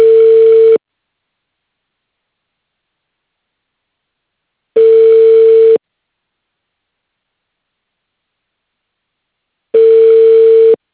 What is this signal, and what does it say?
Ringback tone (China, 450 Hz).